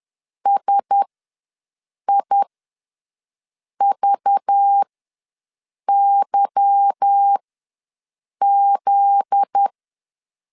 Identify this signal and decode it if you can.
Morse code, heard as a keyed tone, about 10 wpm, spaced as for 6 wpm, 800 Hz. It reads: SIVYZ
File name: LEZIONI / 8 YJ45 / 8 YJ45 ws 10wpm 3843